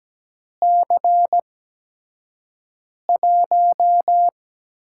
Morse code, heard as a keyed tone, about 17 wpm, spaced as for 5 wpm, 700 Hz. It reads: C1
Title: USA Call Signs - 6 Characters - 8x Character Spacing - 17wpm